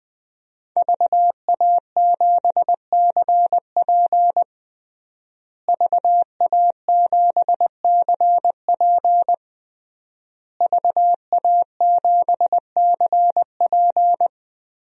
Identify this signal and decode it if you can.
Morse code, heard as a keyed tone, about 20 wpm, 700 Hz. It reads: VA7CP VA7CP VA7CP
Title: DXCC-100 Call Signs - 5 Characters - Repeated 3x - 20wpm